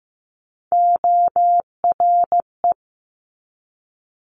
Morse code, heard as a keyed tone, about 15 wpm, 700 Hz. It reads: ORE
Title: Multi-letter Phonemes 15wpm